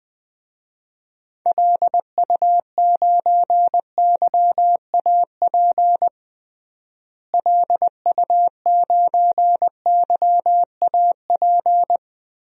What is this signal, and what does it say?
Morse code, heard as a keyed tone, about 21 wpm, 700 Hz. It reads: LU9YAP LU9YAP
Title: DXCC-100 Call Signs - 6 Characters - Repeated 3x - 20wpm